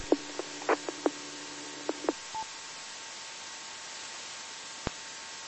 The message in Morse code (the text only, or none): none